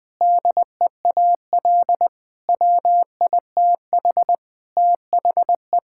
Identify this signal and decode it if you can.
Morse code, heard as a keyed tone, about 21 wpm, 700 Hz. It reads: DEAL WITH THE
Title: Sentences with 4-Word Phrases - 20wpm